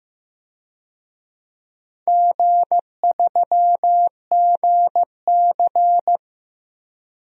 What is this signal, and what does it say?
Morse code, heard as a keyed tone, about 15 wpm, 700 Hz. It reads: G3GC
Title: DXCC-100 Call Signs - 4 Characters - Repeated 3x - 15wpm